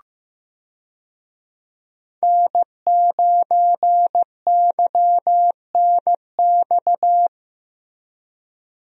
Morse code, heard as a keyed tone, about 15 wpm, 700 Hz. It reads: N9YNX
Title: USA Call Signs - 5 Characters - Repeated 3x - 15wpm